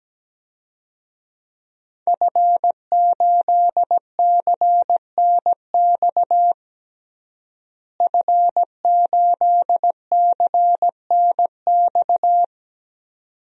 Morse code, heard as a keyed tone, about 17 wpm, 700 Hz. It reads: F8CNX F8CNX
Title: DXCC-100 Call Signs - 5 Characters - Repeated 2x - 17wpm